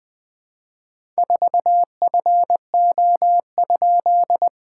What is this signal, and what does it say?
Morse code, heard as a keyed tone, about 20 wpm, 660 Hz. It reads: 4FO?
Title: SOTA QSO Practice 20wpm